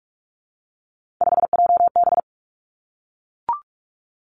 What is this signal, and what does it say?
Morse code, heard as a keyed tone, about 46 wpm, 700 Hz. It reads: 5JB
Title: Three Challenging Letter-Number Combinations 45wpm